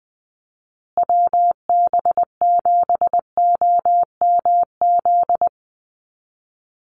Morse code, heard as a keyed tone, about 20 wpm, 700 Hz. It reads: WB7OMZ
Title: USA Call Signs Repeated 2x 20wpm